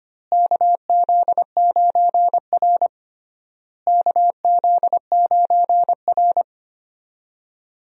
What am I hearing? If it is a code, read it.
Morse code, heard as a keyed tone, about 25 wpm, 700 Hz. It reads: KZ9R KZ9R
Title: USA Call Signs - 4 Characters - Repeated 3x - 25wpm